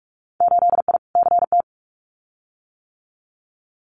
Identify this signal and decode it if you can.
Morse code, heard as a keyed tone, about 46 wpm, 700 Hz. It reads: 8I CT
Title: Winter Field Day Exchanges Encore 45wpm